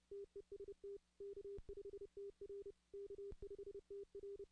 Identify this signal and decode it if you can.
Morse code, heard as a keyed tone, about 30 wpm, 400 Hz. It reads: TEST K5TR K5TR
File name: 2016 arrl dx cw / 2016-ARRL-DX-CW-K5TR- - 0-46